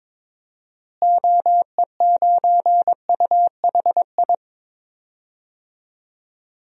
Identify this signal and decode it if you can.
Morse code, heard as a keyed tone, about 22 wpm, 700 Hz. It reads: OE9UHI
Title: DXCC-100 Call Signs - 6 Characters - Repeated 2x - 22wpm